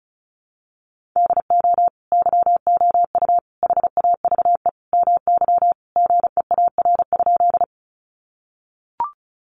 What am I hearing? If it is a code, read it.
Morse code, heard as a keyed tone, about 34 wpm, 700 Hz. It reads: DO YOU HAVE MY GEAR?
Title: Sentences from Top 1400 Words 35wpm